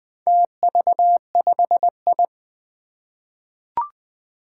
Morse code, heard as a keyed tone, about 21 wpm, 700 Hz. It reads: TV5I
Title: Four Challenging Letter-Number Combinations 20wpm